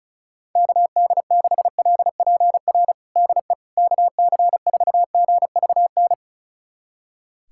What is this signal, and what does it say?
Morse code, heard as a keyed tone, about 34 wpm, 700 Hz. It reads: KD6LPR DE KC4GVN
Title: USA Callsign DE Callsign - 6 Characters - 35wpm